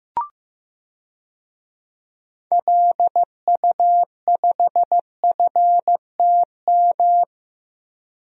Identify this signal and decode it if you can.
Morse code, heard as a keyed tone, about 15 wpm, 700 Hz. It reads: LU5FTM
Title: DXCC-100 Call Signs - 6 Characters - Repeated 2x - 15wpm